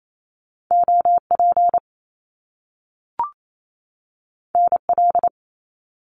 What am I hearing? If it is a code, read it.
Morse code, heard as a keyed tone, about 28 wpm, 700 Hz. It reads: OP NL
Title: Top Two Letter Combinations in Words 28wpm